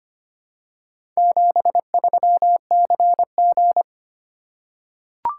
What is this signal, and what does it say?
Morse code, heard as a keyed tone, about 25 wpm, 700 Hz. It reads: 73CG
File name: practice / 25wpm / Four Letter-Number Combinations - 25wpm